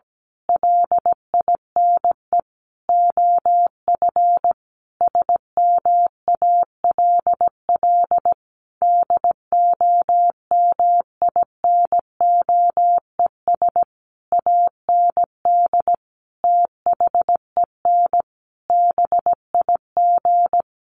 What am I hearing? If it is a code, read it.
Morse code, heard as a keyed tone, about 17 wpm, 700 Hz. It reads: LINE OF SMALL DOMINOES AND THEN BIG
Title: A Sound Of Thunder 17wpm